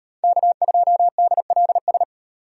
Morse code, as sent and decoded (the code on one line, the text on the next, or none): -.- ..--- -.. .-.. ...
K2DLS